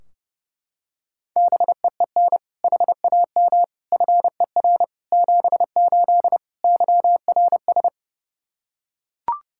Morse code, heard as a keyed tone, about 32 wpm, 700 Hz. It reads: BEEN HAM FER 78 YRS
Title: Practice QSO - Exchange 3 - More Info - 30wpm